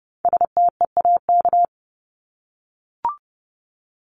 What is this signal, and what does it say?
Morse code, heard as a keyed tone, about 30 wpm, 700 Hz. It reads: STEAK
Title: Top 1251-1500 Words 30wpm